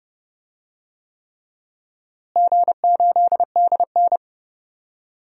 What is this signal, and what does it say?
Morse code, heard as a keyed tone, about 31 wpm, 700 Hz. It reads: G8DN
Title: DXCC-100 Call Signs - 4 Characters - Repeated 3x - 30wpm